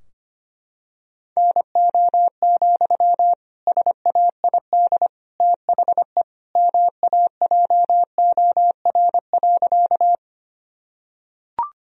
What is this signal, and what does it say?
Morse code, heard as a keyed tone, about 25 wpm, 700 Hz. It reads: NO, SAID THE MAJOR.
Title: Sentences from Top 1000 Words - Review Part 4 - 25wpm